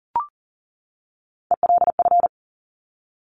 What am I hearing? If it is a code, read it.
Morse code, heard as a keyed tone, about 38 wpm, 700 Hz. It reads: ELF
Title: Top Three Letter Combinations in Words 40wpm